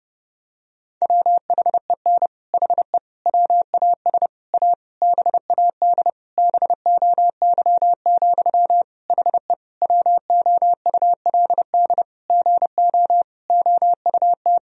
Morse code, heard as a keyed tone, about 30 wpm, 700 Hz. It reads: WHEN HE WAS A BAD BOY, HE WOULD GO OUT
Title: Sentences from Top 1000 Words - Review Part 3 - 30wpm